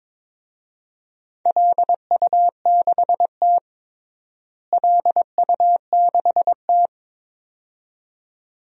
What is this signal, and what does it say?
Morse code, heard as a keyed tone, about 21 wpm, 700 Hz. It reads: LU6T LU6T
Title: DXCC-100 Call Signs - 4 Characters - Repeated 2x - 22wpm